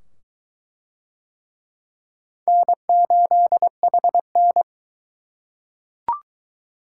Morse code, heard as a keyed tone, about 22 wpm, 700 Hz. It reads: N8HN